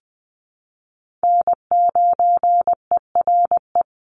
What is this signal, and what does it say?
Morse code, heard as a keyed tone, about 20 wpm, 700 Hz. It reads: N9ERE
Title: USA Call Signs 20wpm